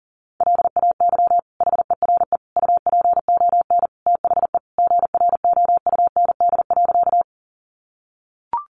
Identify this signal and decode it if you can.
Morse code, heard as a keyed tone, about 40 wpm, 700 Hz. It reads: LAY HERE UPON THE GROUND.
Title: Sentences from Top 1100 Words 40wpm